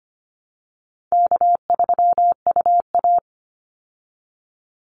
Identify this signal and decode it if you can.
Morse code, heard as a keyed tone, about 25 wpm, 700 Hz. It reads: K3UA
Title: Easy Call Signs Repeated 3x - 25wpm